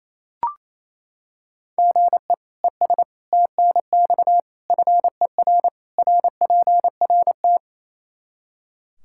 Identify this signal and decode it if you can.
Morse code, heard as a keyed tone, about 28 wpm, 700 Hz. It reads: GE ES TNX FER RPRT
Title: Practice QSO - Exchange 1 - Beginning - 28wpm